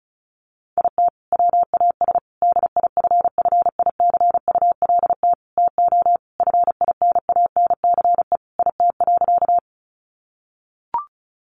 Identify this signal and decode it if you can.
Morse code, heard as a keyed tone, about 34 wpm, 700 Hz. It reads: IT WAS DIFFICULT TO FINANCE IT.
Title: Sentences from Top 1500 Words 35wpm